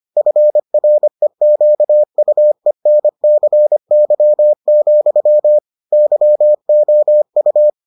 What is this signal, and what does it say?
Morse code, heard as a keyed tone, about 25 wpm, 600 Hz. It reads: FREQUENCY, YOU